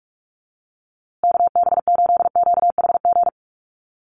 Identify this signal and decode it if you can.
Morse code, heard as a keyed tone, about 44 wpm, 700 Hz. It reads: KB8QHG
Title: USA Call Signs Repeated 2x 45wpm